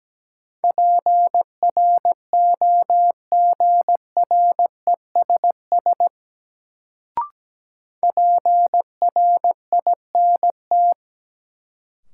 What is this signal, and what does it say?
Morse code, heard as a keyed tone, about 17 wpm, 700 Hz. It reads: PROGRESS PRINT
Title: Sentences from Top 1400 Words - New Word Drill - 17wpm